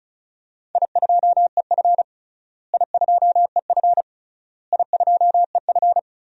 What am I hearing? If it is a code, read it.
Morse code, heard as a keyed tone, about 36 wpm, 700 Hz. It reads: I2EF I2EF I2EF
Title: DXCC-100 Call Signs - 4 Characters - Repeated 3x - 35wpm